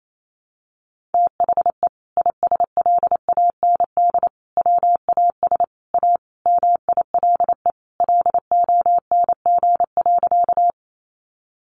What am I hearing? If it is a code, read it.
Morse code, heard as a keyed tone, about 28 wpm, 700 Hz. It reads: THE ISLAND WAS A MILE LONG.
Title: Sentences from Top 500 Words 28wpm